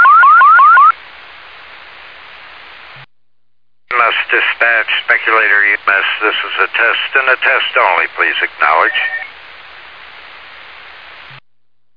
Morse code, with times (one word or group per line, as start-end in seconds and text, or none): none